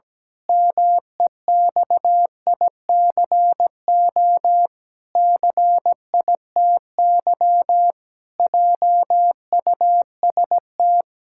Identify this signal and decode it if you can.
Morse code, heard as a keyed tone, about 17 wpm, 700 Hz. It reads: MEXICO CITY JUST
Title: Pangram Challenge v2 - 17wpm